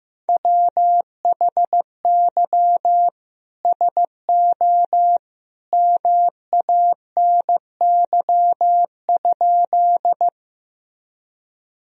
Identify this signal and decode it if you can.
Morse code, heard as a keyed tone, about 16 wpm, 700 Hz. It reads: WHY SO MANY?
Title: Sentences from Top 1000 Words - Review Part 1 - 15wpm